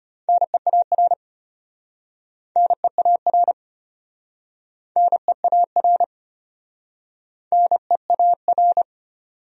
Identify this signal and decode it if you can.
Morse code, heard as a keyed tone, about 36 wpm, 700 Hz. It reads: NEAR NEAR NEAR NEAR